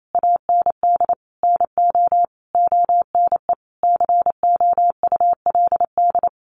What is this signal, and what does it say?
Morse code, heard as a keyed tone, about 28 wpm, 700 Hz. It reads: AND NO ONE COULD
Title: Sentences from Top 1400 Words 28wpm